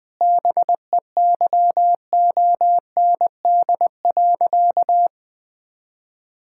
Morse code, heard as a keyed tone, about 21 wpm, 700 Hz. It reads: BEYOND.
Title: Difficult Sentences with Hints - 20wpm